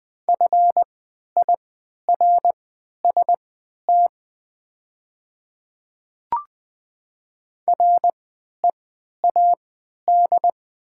Morse code, heard as a keyed tone, about 20 wpm, spaced as for 11 wpm, 690 Hz. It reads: FIRST READ
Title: 3x Character Spacing - Top 500 Words - 20wpm